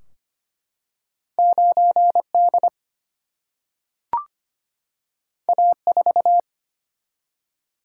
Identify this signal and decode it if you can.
Morse code, heard as a keyed tone, about 25 wpm, 700 Hz. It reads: 9 D A 4
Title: Single-Letter Single-Number 25wpm